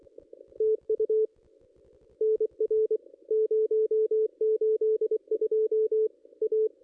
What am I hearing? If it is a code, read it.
Morse code, heard as a keyed tone, about 23 wpm, 440 Hz. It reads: TU NR 082 A
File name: contests / ss-cw-k5tr-op-n5rz -1900